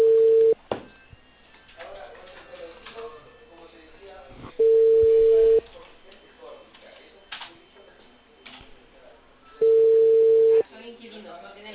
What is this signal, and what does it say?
Telephone call progress tone (China, 450 Hz), heard as a telephone network's tone, about 450 Hz, with speech. Ringback tone.